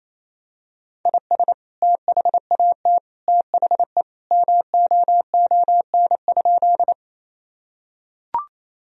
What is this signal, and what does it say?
Morse code, heard as a keyed tone, about 27 wpm, 700 Hz. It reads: IS THAT THE MOON?
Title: Sentences from Top 1000 Words - Review Part 3 - 28wpm